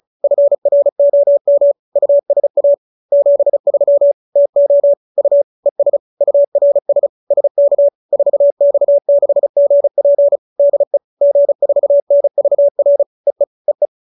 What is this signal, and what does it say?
Morse code, heard as a keyed tone, about 34 wpm, 570 Hz. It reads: FROM USA 73 TO U ES URS SK 4X6GP DE G4NVR EE EE